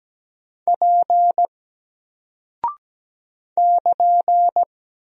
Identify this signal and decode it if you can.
Morse code, heard as a keyed tone, about 17 wpm, 700 Hz. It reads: P <KN>
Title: Single Character (Letter:Number:Punctuation:Prosign) 17wpm